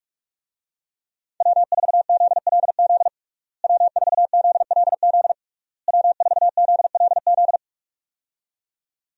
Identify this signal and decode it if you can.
Morse code, heard as a keyed tone, about 46 wpm, 700 Hz. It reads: W4ZLZ W4ZLZ W4ZLZ
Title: USA Call Signs - 5 Characters - Repeated 3x - 45wpm